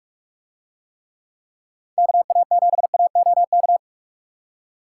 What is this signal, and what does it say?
Morse code, heard as a keyed tone, about 44 wpm, 700 Hz. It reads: KA7AOK